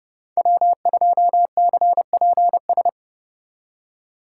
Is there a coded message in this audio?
Yes: Morse code from 0.4 to 2.9 s, heard as a keyed tone, about 34 wpm, 700 Hz. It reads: W2CPS